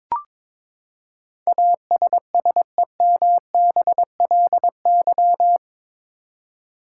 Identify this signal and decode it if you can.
Morse code, heard as a keyed tone, about 21 wpm, 700 Hz. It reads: ASSEMBLY
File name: practice / 22wpm / Eight Letter Words - 22wpm